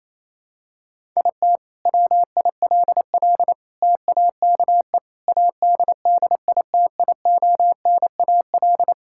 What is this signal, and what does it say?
Morse code, heard as a keyed tone, about 28 wpm, 700 Hz. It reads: IT WILL TAKE ADDITIONAL
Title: Sentences from Top 1000 Words - Review Part 5 - 28wpm